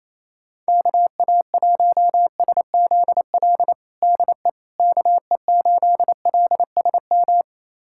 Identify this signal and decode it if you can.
Morse code, heard as a keyed tone, about 28 wpm, 700 Hz. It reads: KA1SZL DE KE8LSM
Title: USA Callsign DE Callsign - 6 Characters - 28wpm